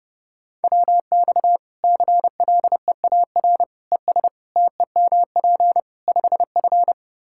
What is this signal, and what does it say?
Morse code, heard as a keyed tone, about 30 wpm, 700 Hz. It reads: WX CLEAR ES TEMP 5F
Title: Practice QSO - Exchange 2 - Info - 30wpm